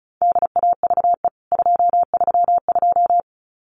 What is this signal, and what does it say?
Morse code, heard as a keyed tone, about 34 wpm, 700 Hz. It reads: DAVE 232
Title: CWT Exchange 35wpm